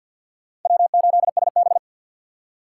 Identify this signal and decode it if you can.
Morse code, heard as a keyed tone, about 64 wpm, 700 Hz. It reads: W8SB